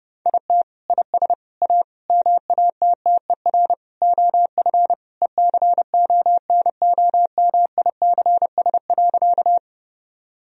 Morse code, heard as a keyed tone, about 30 wpm, 700 Hz. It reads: IT IS A MATTER OF ECONOMICS.
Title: Sentences from Top 1000 Words - Review Part 3 - 30wpm